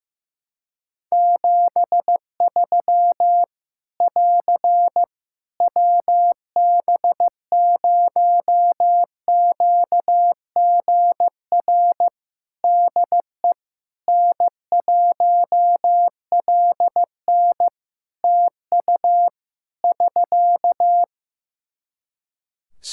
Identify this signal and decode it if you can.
Morse code, heard as a keyed tone, about 15 wpm, 700 Hz. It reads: 73 <AR> WB0QGR DE N1LN TU <SK>
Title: Practice QSO - Exchange 4 - Ending - 15wpm